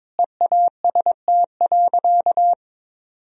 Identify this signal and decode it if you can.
Morse code, heard as a keyed tone, about 22 wpm, 700 Hz. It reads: EAST.
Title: Sentences from Top 1000 Words - Review Part 3 - 22wpm